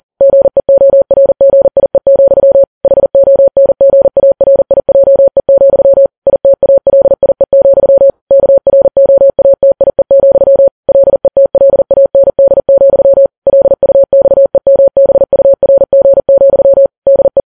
Morse code, heard as a keyed tone, about 40 wpm, 550 Hz. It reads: GEORGIE, HONGARIJE, ITALIE, KROATIE, LETLAND, LUXEMBURG, DE
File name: Files Web tls / Morse on9fv / morse40kps